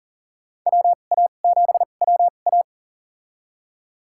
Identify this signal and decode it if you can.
Morse code, heard as a keyed tone, about 40 wpm, spaced as for 29 wpm, 700 Hz. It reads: WA7WA
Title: USA Call Signs - 5 Characters - 2x Character Spacing - 40wpm